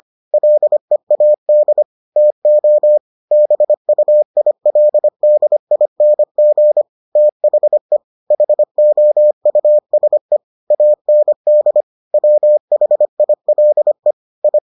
Morse code, heard as a keyed tone, about 25 wpm, 600 Hz. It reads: LEAD TO BUILDING THE HOUSE AND WHILE I